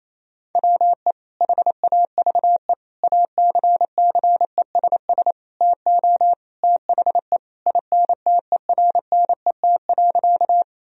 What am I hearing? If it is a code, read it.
Morse code, heard as a keyed tone, about 28 wpm, 700 Hz. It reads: WE HAVE ACCESS TO THE INTERNET.
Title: Sentences from Top 1000 Words - Review Part 4 - 28wpm